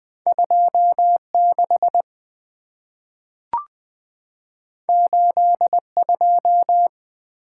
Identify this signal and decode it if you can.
Morse code, heard as a keyed tone, about 20 wpm, 700 Hz. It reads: 2 6 8 2
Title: Two Letter-Number Combinations - 20wpm